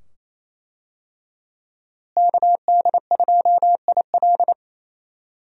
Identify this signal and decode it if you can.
Morse code, heard as a keyed tone, about 28 wpm, 700 Hz. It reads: KD2IL
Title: USA Call Signs - 5 Characters - Repeated 2x - 28wpm